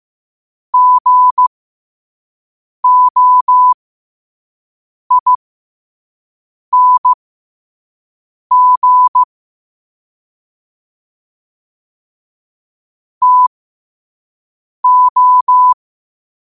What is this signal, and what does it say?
Morse code, heard as a keyed tone, about 15 wpm, spaced as for 5 wpm, 1000 Hz. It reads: GOING TO
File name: qotdfiles / 5wpm